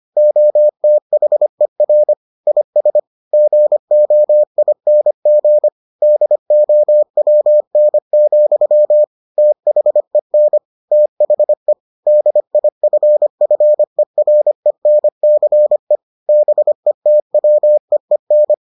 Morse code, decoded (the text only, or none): OTHER IS GOING DOWN, THEN THE DIFFERENCE BETWEEN